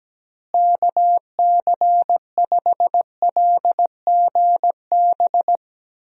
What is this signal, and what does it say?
Morse code, heard as a keyed tone, about 17 wpm, 700 Hz. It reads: KC5LGB
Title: USA Call Signs - 6 Characters - Repeated 3x - 17wpm